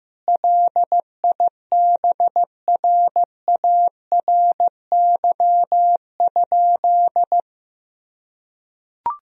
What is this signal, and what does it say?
Morse code, heard as a keyed tone, about 16 wpm, 700 Hz. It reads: LIBRARY?
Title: Sentences from Top 1000 Words - Review Part 4 - 15wpm